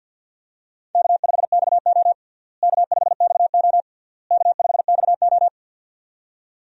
Morse code, heard as a keyed tone, about 51 wpm, 700 Hz. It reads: K5XO K5XO K5XO
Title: USA Call Signs - 4 Characters - Repeated 3x - 50wpm